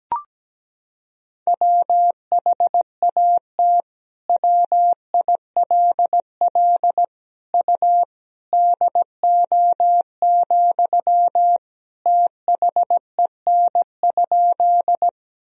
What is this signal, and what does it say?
Morse code, heard as a keyed tone, about 17 wpm, 700 Hz. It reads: WHAT WILL U DO, THEN?
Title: Sentences from Top 100 Words and Abbreviations 17wpm